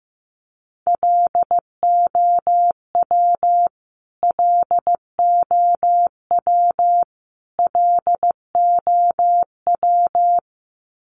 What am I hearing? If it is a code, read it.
Morse code, heard as a keyed tone, about 15 wpm, 700 Hz. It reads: LOW LOW LOW
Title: Top 500 Words Repeated 15wpm